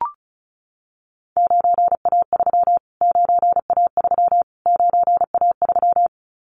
Morse code, heard as a keyed tone, about 34 wpm, 700 Hz. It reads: 9A3 9A3 9A3
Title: Callsign Prefixes for Top 25 DXCC - Repeated 3x - 35wpm